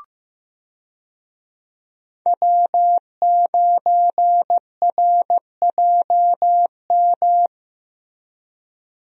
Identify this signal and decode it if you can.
Morse code, heard as a keyed tone, about 15 wpm, 700 Hz. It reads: W9RJM
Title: USA Call Signs - 5 Characters - Repeated 2x - 15wpm